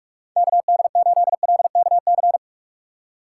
Morse code, heard as a keyed tone, about 46 wpm, 700 Hz. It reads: KD8LKC
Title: USA Call Signs - 6 Characters - 3x Character Spacing - 45wpm